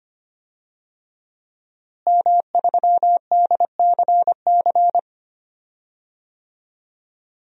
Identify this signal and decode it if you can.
Morse code, heard as a keyed tone, about 25 wpm, 700 Hz. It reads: M3DCC
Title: DXCC-100 Call Signs - 5 Characters - Repeated 3x - 25wpm